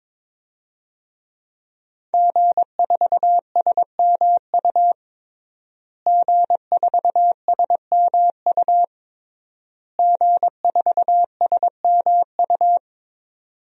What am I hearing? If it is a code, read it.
Morse code, heard as a keyed tone, about 21 wpm, 700 Hz. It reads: G4SMU G4SMU G4SMU
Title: DXCC-100 Call Signs - 5 Characters - Repeated 3x - 22wpm